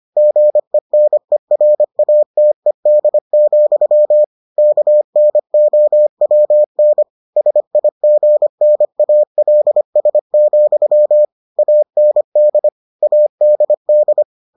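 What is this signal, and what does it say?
Morse code, heard as a keyed tone, about 25 wpm, 600 Hz. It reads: GENERATED, KNOWN SIGNALS, AND ADD
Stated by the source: Morse edition of the Foundations of Amateur Radio episodes.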